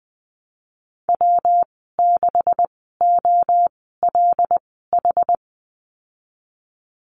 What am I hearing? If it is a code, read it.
Morse code, heard as a keyed tone, about 20 wpm, spaced as for 15 wpm, 700 Hz. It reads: W6OLH
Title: 147 Lesson 24g - Letter y - Call Signs 20wpm